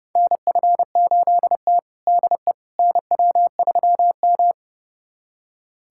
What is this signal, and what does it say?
Morse code, heard as a keyed tone, about 30 wpm, 700 Hz. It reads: NF8T DE NW3M
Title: USA Callsign DE Callsign - 4 Characters - 30wpm